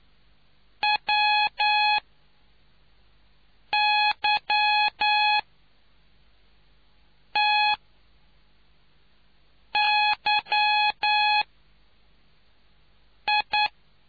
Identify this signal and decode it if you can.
Morse code, heard as a keyed tone, about 9 wpm, spaced as for 4 wpm, 840 Hz. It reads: WYTYI